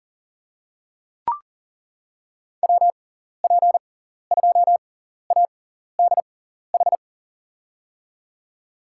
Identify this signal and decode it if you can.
Morse code, heard as a keyed tone, about 40 wpm, spaced as for 14 wpm, 700 Hz. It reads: WP2ADH
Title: USA Call Signs - 6 Characters - 6x Character Spacing - 40wpm